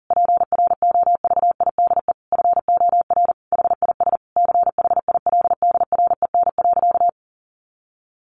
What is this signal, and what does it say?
Morse code, heard as a keyed tone, about 40 wpm, 700 Hz. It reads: PROVIDE FOR HIS CHILDREN.
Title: Sentences from Top 1100 Words 40wpm